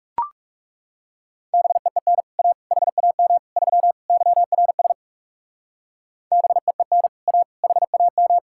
Morse code, heard as a keyed tone, about 34 wpm, 700 Hz. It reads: BEEN A HAM 3 YRS BEEN A HAM
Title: Speed-Racing Common QSO Phrases - 30wpm